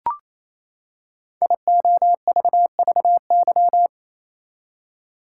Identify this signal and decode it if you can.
Morse code, heard as a keyed tone, about 28 wpm, 700 Hz. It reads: IOVVY
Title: Code Groups - 5 Letter:Numbers - 28wpm